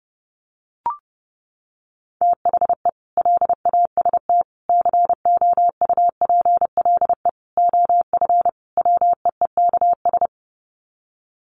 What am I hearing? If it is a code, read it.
Morse code, heard as a keyed tone, about 30 wpm, 700 Hz. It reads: THE LAST COUPLE OF WEEKS
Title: Sets of 5 Words Encore2 30wpm